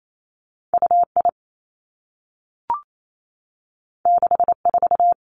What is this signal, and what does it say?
Morse code, heard as a keyed tone, about 28 wpm, 700 Hz. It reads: U I 6 4
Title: Two Challenging Letter-Number Combinations 28wpm